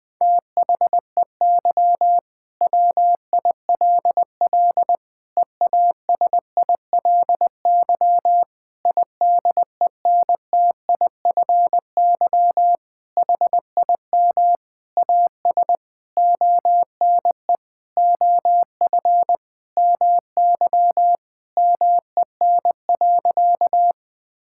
Morse code, heard as a keyed tone, about 20 wpm, 700 Hz. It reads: THEY WILL EASILY IDENTIFY HIM AS ONE OF MY MEN.
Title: Sentences from Top 1000 Words - Review Part 6 - 20wpm